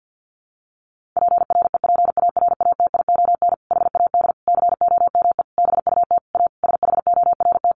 Morse code, heard as a keyed tone, about 50 wpm, 700 Hz. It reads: PREPARATION HAD COME BUT A SHORT
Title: Sentences from Top 1400 Words 50wpm